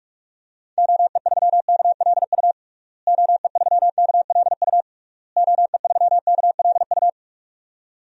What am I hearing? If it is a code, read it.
Morse code, heard as a keyed tone, about 46 wpm, 700 Hz. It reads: OE3KLU OE3KLU OE3KLU
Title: DXCC-100 Call Signs - 6 Characters - Repeated 3x - 45wpm